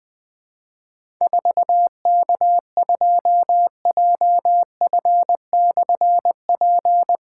Morse code, heard as a keyed tone, about 20 wpm, 700 Hz. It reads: 4K2JF/P
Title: Difficult Call Signs 20wpm